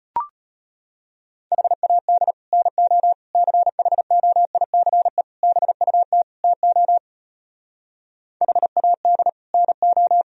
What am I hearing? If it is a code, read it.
Morse code, heard as a keyed tone, about 33 wpm, 700 Hz. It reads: HAD NO CHOICE BUT TO HAD NO